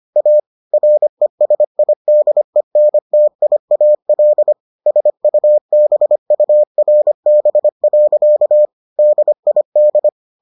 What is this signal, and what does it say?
Morse code, heard as a keyed tone, about 25 wpm, 600 Hz. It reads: A RESIDENTIAL SUBURB. DID